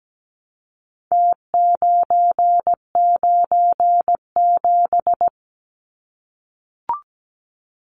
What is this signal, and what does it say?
Morse code, heard as a keyed tone, about 17 wpm, 700 Hz. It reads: T997
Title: Numbers - Contest Style - 4-Digit 17wpm